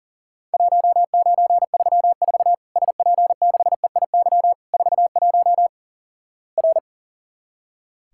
Morse code, heard as a keyed tone, about 43 wpm, 700 Hz. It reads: 1934 SP6EIY 41 R
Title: Worked All Europe DX Contest - QTC Exchange - 40wpm